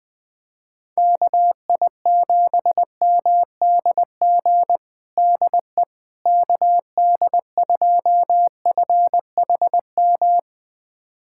Morse code, heard as a keyed tone, about 21 wpm, 700 Hz. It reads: KI7MDG DE KD2FHM
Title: USA Callsign DE Callsign 20wpm